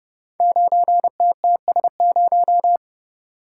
Morse code, heard as a keyed tone, about 30 wpm, 700 Hz. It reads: 9TTS0
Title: Code Groups - 5 Letter:Numbers - 30wpm